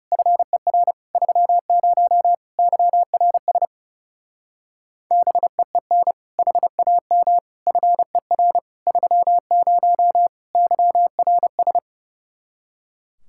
Morse code, heard as a keyed tone, about 37 wpm, 700 Hz. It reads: FER 30 YRS BEEN HAM FER 30 YRS
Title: Speed-Racing Common QSO Phrases - 30wpm